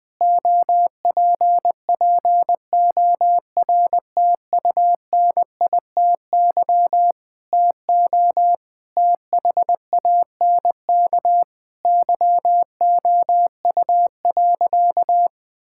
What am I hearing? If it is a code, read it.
Morse code, heard as a keyed tone, about 20 wpm, 700 Hz. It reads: OPPORTUNITY TO THANK YOU.
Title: Sentences with 7-Word Phrases - Spoken Beginning - 20wpm